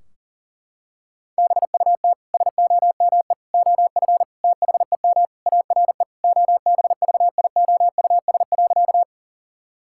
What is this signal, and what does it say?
Morse code, heard as a keyed tone, about 37 wpm, 700 Hz. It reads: BUT SOME OF THEM ARE OBVIOUS.
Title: Sentences from Top 1600 Words 40wpm